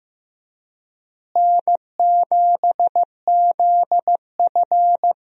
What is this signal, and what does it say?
Morse code, heard as a keyed tone, about 15 wpm, 700 Hz. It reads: N7ZF
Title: USA Call Signs - 4 Characters - Repeated 2x - 15wpm